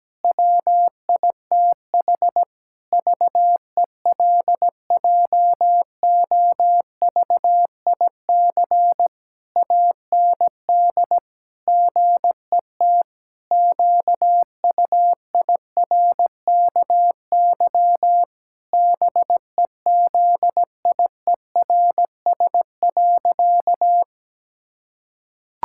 Morse code, heard as a keyed tone, about 17 wpm, 700 Hz. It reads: WITH VELJOVIC AND GET QUIRKY BEZIERS.
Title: Pangram Challenge v2 - 17wpm